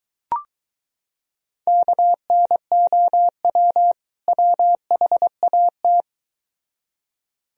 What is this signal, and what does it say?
Morse code, heard as a keyed tone, about 23 wpm, 700 Hz. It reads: KNOW WHAT